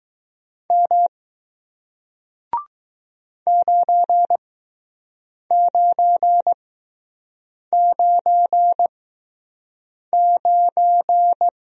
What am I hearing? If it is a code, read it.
Morse code, heard as a keyed tone, about 20 wpm, spaced as for 6 wpm, 700 Hz. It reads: M 9999